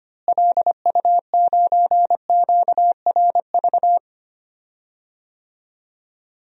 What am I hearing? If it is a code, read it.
Morse code, heard as a keyed tone, about 25 wpm, 700 Hz. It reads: LU9QRV
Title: DXCC-100 Call Signs - 6 Characters - Repeated 2x - 25wpm